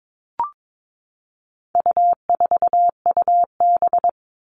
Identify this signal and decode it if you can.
Morse code, heard as a keyed tone, about 22 wpm, 700 Hz. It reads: U4UB
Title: All Cut Numbers - 4-Digit 22wpm